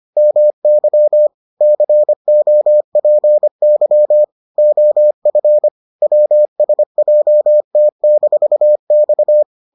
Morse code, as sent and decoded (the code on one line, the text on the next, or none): -- -.-- / -.-. --- .--. -.-- / --- ..-. / .-- ... .--- - -....- -..-
MY COPY OF WSJT-X